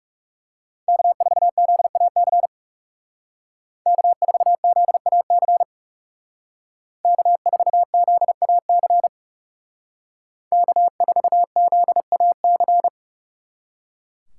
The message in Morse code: K4ZAC K4ZAC K4ZAC K4ZAC

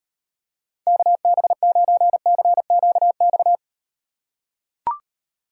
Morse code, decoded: KB9CQX

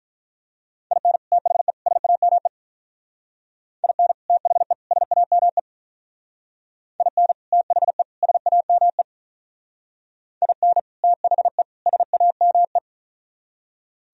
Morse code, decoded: IN THE SAME IN THE SAME IN THE SAME IN THE SAME

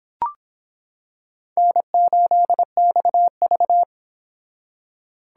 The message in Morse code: N8XV

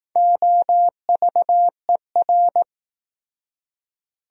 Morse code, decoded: OVER

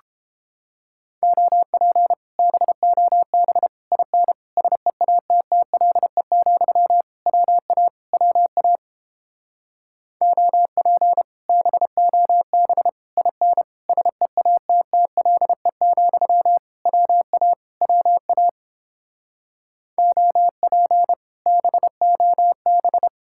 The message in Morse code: OP BOB IN SEATTLE, WA WA OP BOB IN SEATTLE, WA WA OP BOB

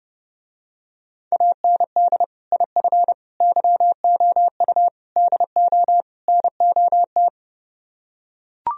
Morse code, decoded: AND IF YOU DO NOT

